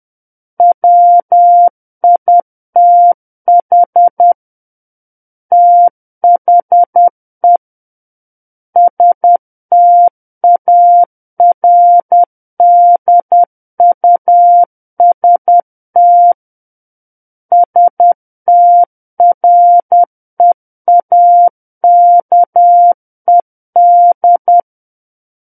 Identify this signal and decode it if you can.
Morse code, heard as a keyed tone, about 10 wpm, 700 Hz. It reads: WITH THE STARDUST STREAKED
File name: Book One / 01-Chapter 1 - 10 WPM